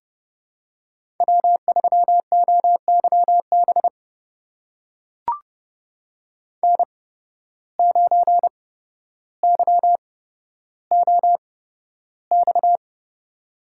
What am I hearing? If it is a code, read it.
Morse code, heard as a keyed tone, about 30 wpm, spaced as for 13 wpm, 700 Hz. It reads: W3OYB N9YOX
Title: USA Call Signs - 5 Characters - 8x Character Spacing - 30wpm